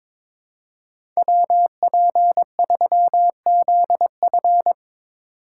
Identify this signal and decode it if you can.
Morse code, heard as a keyed tone, about 22 wpm, 700 Hz. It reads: WP3ZF
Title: USA Call Signs - 5 Characters - 3x Character Spacing - 22wpm